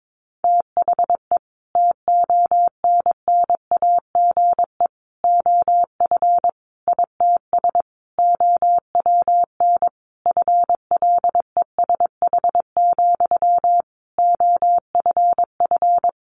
Morse code, heard as a keyed tone, about 22 wpm, 700 Hz. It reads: THE TONNAGE OF ITS OWN FLESH, OFF
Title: A Sound Of Thunder 22wpm